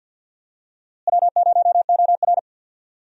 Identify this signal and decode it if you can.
Morse code, heard as a keyed tone, about 53 wpm, 700 Hz. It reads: W0OR
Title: USA Call Signs - 4 Characters - Repeated 3x - 50wpm